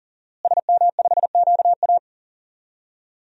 Morse code, heard as a keyed tone, about 40 wpm, 690 Hz. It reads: SM5QA
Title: DXCC-100 Call Signs - 5 Characters - Repeated 2x - 40wpm